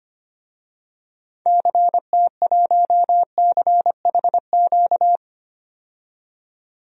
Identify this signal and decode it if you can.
Morse code, heard as a keyed tone, about 25 wpm, 700 Hz. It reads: CT1CHQ
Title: DXCC-100 Call Signs - 6 Characters - 25wpm